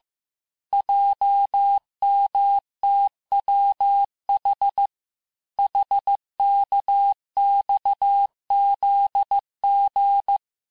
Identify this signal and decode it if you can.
Morse code, heard as a keyed tone, about 15 wpm, 800 Hz. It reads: JMTWH HKXZG
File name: cours-du-lundi files / cours-1585-texte-chiffre-vitesse-14mpm